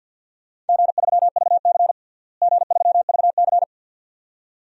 Morse code, heard as a keyed tone, about 53 wpm, 700 Hz. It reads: G3VC G3VC
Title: DXCC-100 Call Signs - 4 Characters - Repeated 2x - 50wpm